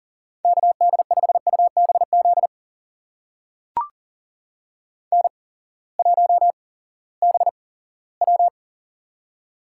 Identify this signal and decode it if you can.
Morse code, heard as a keyed tone, about 42 wpm, spaced as for 20 wpm, 690 Hz. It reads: KD5UBZ N1BW
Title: USA Call Signs - 8x Character Spacing - 40wpm